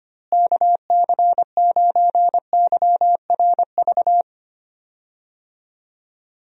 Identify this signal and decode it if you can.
Morse code, heard as a keyed tone, about 25 wpm, 700 Hz. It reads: KC9YRV
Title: USA Call Signs - 6 Characters - Repeated 2x - 25wpm